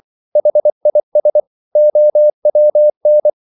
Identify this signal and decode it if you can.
Morse code, heard as a keyed tone, about 24 wpm, 600 Hz. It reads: HIS OWN